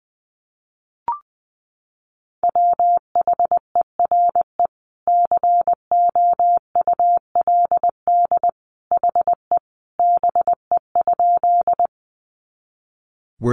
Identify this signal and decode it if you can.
Morse code, heard as a keyed tone, about 20 wpm, 700 Hz. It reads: WHERE COULD HE BE?
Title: Sentences from Top 200 Words 20wpm